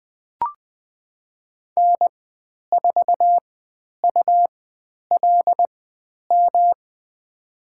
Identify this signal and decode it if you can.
Morse code, heard as a keyed tone, about 20 wpm, spaced as for 10 wpm, 700 Hz. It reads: N4ULM
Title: 109 Lesson 18g - Letter m - Call Signs 20wpm